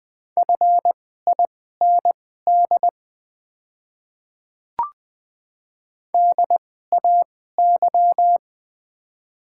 Morse code, heard as a keyed tone, about 20 wpm, spaced as for 14 wpm, 690 Hz. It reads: FIND DAY
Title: 2x Character Spacing - Top 100 Words - 20wpm